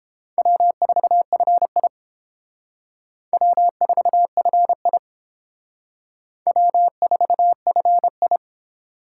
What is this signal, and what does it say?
Morse code, heard as a keyed tone, about 30 wpm, 700 Hz. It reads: W4FI W4FI W4FI